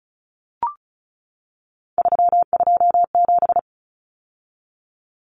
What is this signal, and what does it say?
Morse code, heard as a keyed tone, about 34 wpm, 700 Hz. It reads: 327
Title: Common Cut Numbers - 3-Digit 35wpm